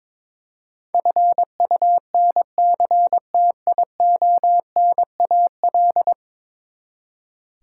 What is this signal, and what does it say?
Morse code, heard as a keyed tone, about 22 wpm, 700 Hz. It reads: FUNCTIONAL
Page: morse-code-ninja-practice-22wpm